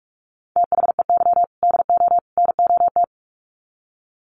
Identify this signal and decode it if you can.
Morse code, heard as a keyed tone, about 44 wpm, 700 Hz. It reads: THEY DO NOT
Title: Sets of 3 Words 45wpm